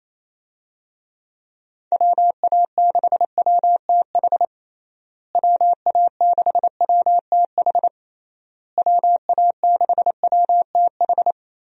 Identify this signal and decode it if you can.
Morse code, heard as a keyed tone, about 28 wpm, 700 Hz. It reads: WA6WTH WA6WTH WA6WTH
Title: USA Call Signs - 6 Characters - Repeated 3x - 28wpm